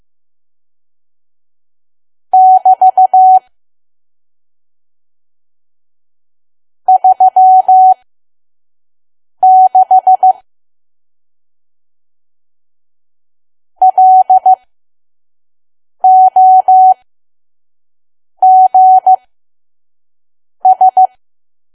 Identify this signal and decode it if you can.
Morse code, heard as a keyed tone, about 15 wpm, spaced as for 5 wpm, 750 Hz. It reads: <BT> 36 LOGS
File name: Archive / 5 WPM files / PM